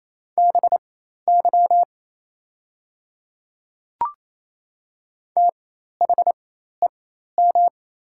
Morse code, heard as a keyed tone, about 28 wpm, spaced as for 13 wpm, 690 Hz. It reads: BY THEM
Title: 4x Character Spacing - Top 200 Words - 28wpm